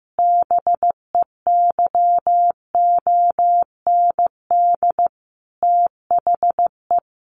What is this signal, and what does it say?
Morse code, heard as a keyed tone, about 15 wpm, 700 Hz. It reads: BEYOND THE
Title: A Sound Of Thunder 15wpm